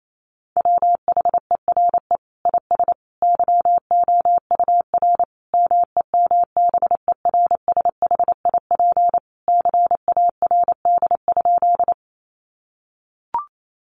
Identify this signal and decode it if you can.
Morse code, heard as a keyed tone, about 28 wpm, 700 Hz. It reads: WHERE IS YOUR MEMBERSHIP CARD?
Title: Sentences from Top 1400 Words 28wpm